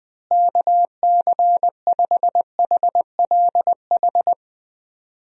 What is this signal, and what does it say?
Morse code, heard as a keyed tone, about 21 wpm, 700 Hz. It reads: KC5HLH
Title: USA Call Signs - 6 Characters - Repeated 2x - 20wpm